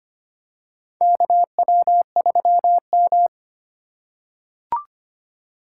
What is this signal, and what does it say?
Morse code, heard as a keyed tone, about 25 wpm, 700 Hz. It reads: KW3M
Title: USA Call Signs - 4 Characters - 2x Character Spacing - 25wpm